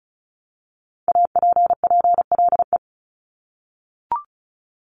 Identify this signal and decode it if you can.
Morse code, heard as a keyed tone, about 36 wpm, 700 Hz. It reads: APPLE
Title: Multi-letter Phonemes Repeated 3x 35wpm